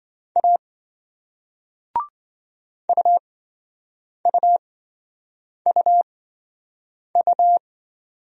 Morse code, heard as a keyed tone, about 28 wpm, spaced as for 7 wpm, 700 Hz. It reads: A UUUU